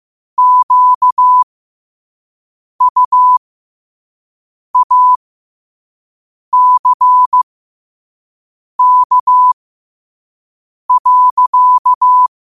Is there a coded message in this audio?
Yes: Morse code from 0.4 to 12.3 s, heard as a keyed tone, about 15 wpm, spaced as for 5 wpm, 1000 Hz. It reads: QUACK.